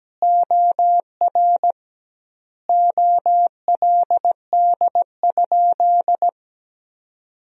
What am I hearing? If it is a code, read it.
Morse code, heard as a keyed tone, about 17 wpm, 700 Hz. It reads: OR OLD?
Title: 2x Word-Spacing - Sentences from Top 400 Words - 17wpm